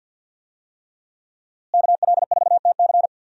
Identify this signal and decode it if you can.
Morse code, heard as a keyed tone, about 51 wpm, 700 Hz. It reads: KL4TC
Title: USA Call Signs - 5 Characters - Repeated 3x - 50wpm